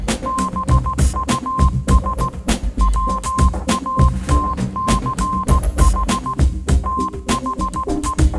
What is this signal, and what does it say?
Morse code, heard as a keyed tone, about 16 wpm, 1050 Hz. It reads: BUS WATCH THE
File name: Kirkoid / Kirkoid - Small Stuff (Morse Mix)